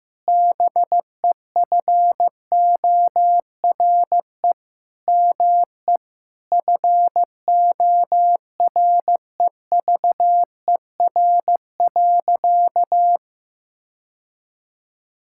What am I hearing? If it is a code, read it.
Morse code, heard as a keyed tone, about 15 wpm, 700 Hz. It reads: BEFORE ME FOREVER.
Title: Sentences from Top 1600 Words 15wpm